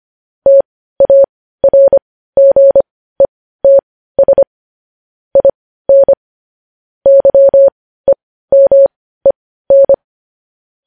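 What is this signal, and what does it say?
Morse code, heard as a keyed tone, about 25 wpm, spaced as for 15 wpm, 550 Hz. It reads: TARGETS IN YEMEN
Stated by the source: Morse news headlines practice at 15.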